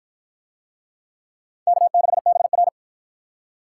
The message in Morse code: K6BR

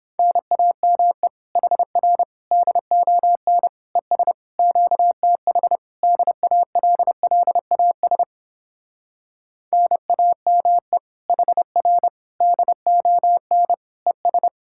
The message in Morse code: NAME HR DON ES QTH DALLAS NAME HR DON ES

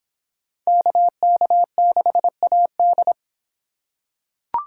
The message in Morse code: KK6AD

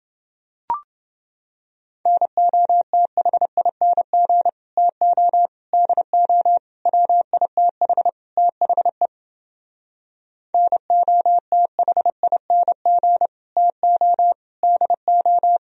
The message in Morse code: NOTHING TO DO WITH THE NOTHING TO DO

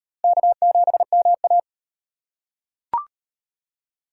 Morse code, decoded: K7MA

